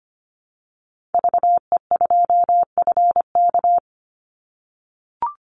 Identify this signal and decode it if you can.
Morse code, heard as a keyed tone, about 25 wpm, 700 Hz. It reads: VE2FK
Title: Call Signs - 25wpm